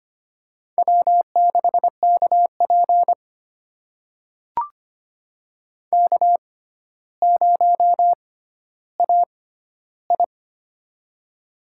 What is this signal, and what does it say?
Morse code, heard as a keyed tone, about 25 wpm, spaced as for 13 wpm, 700 Hz. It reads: W6KP K0AI
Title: USA Call Signs - 4 Characters - 6x Character Spacing - 25wpm